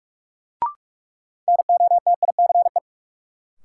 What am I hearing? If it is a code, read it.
Morse code, heard as a keyed tone, about 46 wpm, 700 Hz. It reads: NOTICE
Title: Sentences from Top 400 Words - New Word Drill - 45wpm